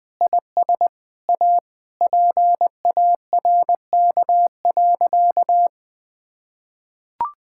Morse code, heard as a keyed tone, about 20 wpm, 700 Hz. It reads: IS A PARK.
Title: Sentences with 5-Word Phrases - 20wpm